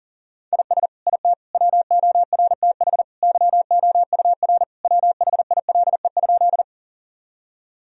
Morse code, heard as a keyed tone, about 37 wpm, 700 Hz. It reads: IS IT WORTH YOUR WHILE?
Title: Sentences from Top 1000 Words - Review Part 1 - 40wpm